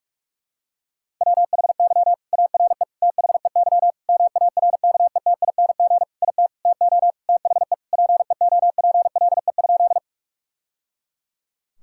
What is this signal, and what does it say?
Morse code, heard as a keyed tone, about 47 wpm, 700 Hz. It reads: WHY ARE THEY MARKETING IT TO THE PEOPLE?
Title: Sentences from Top 1000 Words - Review Part 5 - 45wpm